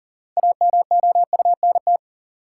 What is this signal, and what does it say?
Morse code, heard as a keyed tone, about 40 wpm, 700 Hz. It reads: AMOUNT
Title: Sentences from Top 700 Words - New Word Drill - 40wpm